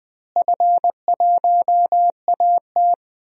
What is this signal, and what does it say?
Morse code, heard as a keyed tone, about 20 wpm, 700 Hz. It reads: F1AT
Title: DXCC-100 Call Signs - 4 Characters - Repeated 3x - 20wpm